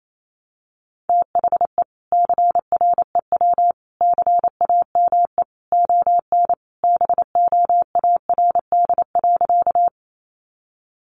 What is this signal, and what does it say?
Morse code, heard as a keyed tone, about 28 wpm, 700 Hz. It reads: THE CREW CAME ON BOARD.
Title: Sentences from Top 1500 Words 28wpm